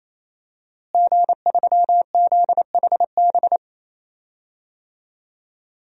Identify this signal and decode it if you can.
Morse code, heard as a keyed tone, about 28 wpm, 700 Hz. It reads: G3ZHB
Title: DXCC-100 Call Signs - 5 Characters - Repeated 3x - 28wpm